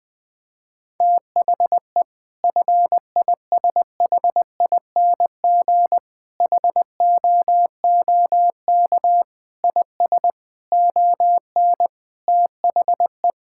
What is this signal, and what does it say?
Morse code, heard as a keyed tone, about 21 wpm, 700 Hz. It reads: THE FISHING HOOK IS ON THE
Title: Sentences from Top 1600 Words 20wpm